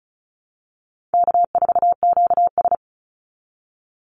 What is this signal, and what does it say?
Morse code, heard as a keyed tone, about 34 wpm, 700 Hz. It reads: K4QS
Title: NA SOTA Activator Callsigns 35wpm